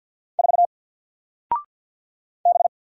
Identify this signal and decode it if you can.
Morse code, heard as a keyed tone, about 50 wpm, 700 Hz. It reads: 4 B
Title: Single Letter-Number Alt 50wpm